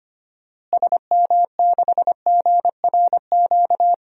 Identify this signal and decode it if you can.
Morse code, heard as a keyed tone, about 25 wpm, 700 Hz. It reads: SM6GRQ
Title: DXCC-100 Call Signs - 6 Characters - Repeated 2x - 25wpm